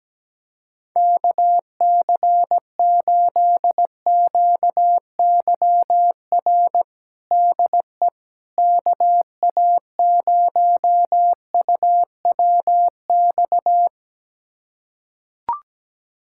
Morse code, heard as a keyed tone, about 17 wpm, 700 Hz. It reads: KC8QYR DE KA0UWX
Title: USA Callsign DE Callsign - 6 Characters - 17wpm